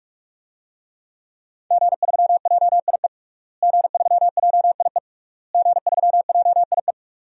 Morse code, heard as a keyed tone, about 46 wpm, 700 Hz. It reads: G3JIE G3JIE G3JIE
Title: DXCC-100 Call Signs - 5 Characters - Repeated 3x - 45wpm